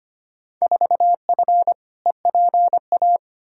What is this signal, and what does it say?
Morse code, heard as a keyed tone, about 25 wpm, 700 Hz. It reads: 4F EPA
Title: Field Day Exchanges - 25wpm